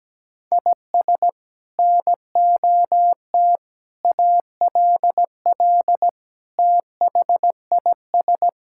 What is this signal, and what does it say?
Morse code, heard as a keyed tone, about 17 wpm, 700 Hz. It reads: IS NOT ALL THIS
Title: Sentences from Top 1600 Words 17wpm